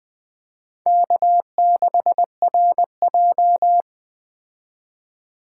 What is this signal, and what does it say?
Morse code, heard as a keyed tone, about 21 wpm, 700 Hz. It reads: K6RJ
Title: Speed Racing - USA 1x2 Call Signs 17wpm